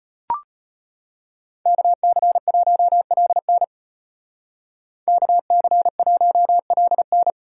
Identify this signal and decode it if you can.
Morse code, heard as a keyed tone, about 36 wpm, 700 Hz. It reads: KC1LN KC1LN